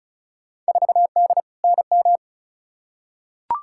Morse code, heard as a keyed tone, about 36 wpm, 700 Hz. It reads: 4D NM
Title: Field Day Exchanges Encore - 35wpm